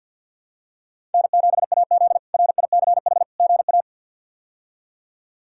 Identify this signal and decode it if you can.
Morse code, heard as a keyed tone, about 50 wpm, 700 Hz. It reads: N7AZ RICH GA
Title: K1USN SST QSO Practice - 50wpm